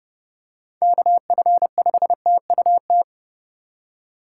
Morse code, heard as a keyed tone, about 31 wpm, 700 Hz. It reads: KF5TUT
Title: USA Call Signs - 6 Characters - 3x Character Spacing - 30wpm